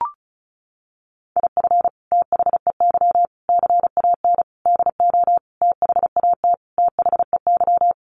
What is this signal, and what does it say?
Morse code, heard as a keyed tone, about 34 wpm, 700 Hz. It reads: IF THEY CAN DO THAT THEY
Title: Sentences from Top 600 Words 35wpm